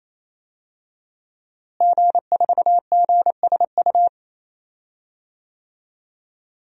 Morse code, heard as a keyed tone, about 28 wpm, 700 Hz. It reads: G4GSU
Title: DXCC-100 Call Signs - 5 Characters - Repeated 3x - 28wpm